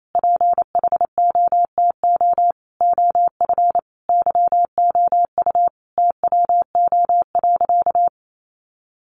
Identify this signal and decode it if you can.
Morse code, heard as a keyed tone, about 28 wpm, 700 Hz. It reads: PHOTO OF YOU TWO.
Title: Sentences from Top 1200 Words 28wpm